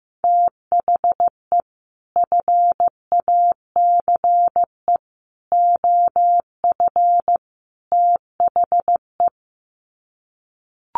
Morse code, heard as a keyed tone, about 15 wpm, 700 Hz. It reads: THE FACE OF THE
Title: Sets of 4 Words 15wpm